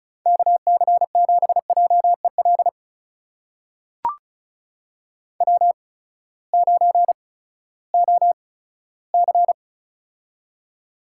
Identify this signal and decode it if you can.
Morse code, heard as a keyed tone, about 36 wpm, spaced as for 18 wpm, 700 Hz. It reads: KC7JEL W9OC
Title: USA Call Signs - 8x Character Spacing - 35wpm